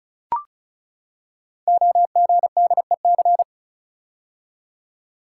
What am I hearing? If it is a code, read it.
Morse code, heard as a keyed tone, about 36 wpm, 700 Hz. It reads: OGDEC
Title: Code Groups - 5 Letters - 35wpm